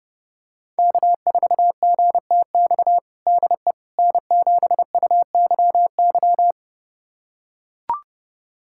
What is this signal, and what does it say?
Morse code, heard as a keyed tone, about 30 wpm, 700 Hz. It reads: K4GTX DE N7UYY
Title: USA Callsign DE Callsign 30wpm